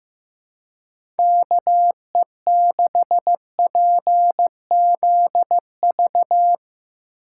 Morse code, heard as a keyed tone, about 15 wpm, 700 Hz. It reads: KE6PZV
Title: USA Call Signs - 6 Characters - 3x Character Spacing - 15wpm